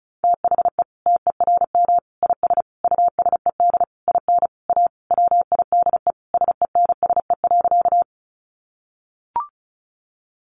Morse code, heard as a keyed tone, about 34 wpm, 700 Hz. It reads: THE TERM IS USED IN A WIDE SENSE.
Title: Sentences from Top 900 Words 35wpm